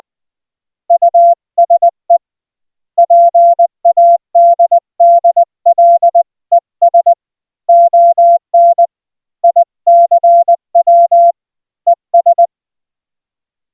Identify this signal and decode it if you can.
Morse code, heard as a keyed tone, about 20 wpm, 690 Hz. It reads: USE PADDLES ON ICW ES